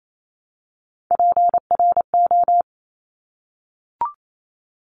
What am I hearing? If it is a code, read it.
Morse code, heard as a keyed tone, about 28 wpm, 700 Hz. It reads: PRO
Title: Top Three Letter Combinations in Words 28wpm